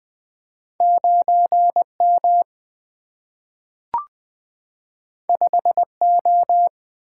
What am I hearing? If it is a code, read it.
Morse code, heard as a keyed tone, about 20 wpm, 700 Hz. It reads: 9 M 5 O
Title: Single-Letter Single-Number 20wpm